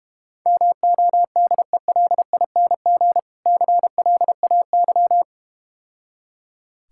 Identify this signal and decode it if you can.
Morse code, heard as a keyed tone, about 32 wpm, 700 Hz. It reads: MODELING CLAY